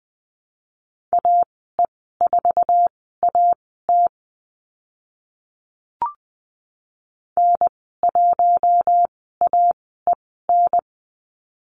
Morse code, heard as a keyed tone, about 20 wpm, spaced as for 14 wpm, 700 Hz. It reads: AE4AT N1AEN
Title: 055 Lesson 10g - Letter r - Call Signs 20wpm